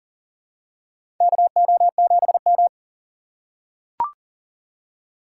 Morse code, heard as a keyed tone, about 40 wpm, 690 Hz. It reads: KO7M
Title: USA Call Signs - 4 Characters - 2x Character Spacing - 40wpm